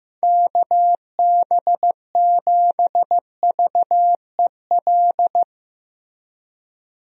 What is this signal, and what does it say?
Morse code, heard as a keyed tone, about 15 wpm, 700 Hz. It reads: KB7VEL
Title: USA Call Signs - 6 Characters - Repeated 3x - 15wpm